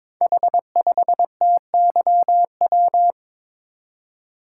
Morse code, heard as a keyed tone, about 22 wpm, 700 Hz. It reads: H5TYW
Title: Code Groups - 5 Letter:Numbers - 22wpm